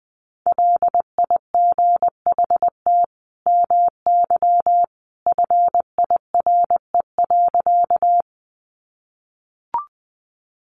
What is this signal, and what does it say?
Morse code, heard as a keyed tone, about 19 wpm, 700 Hz. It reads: LIGHT MY FIRE.
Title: Sentences from Top 400 Words 20wpm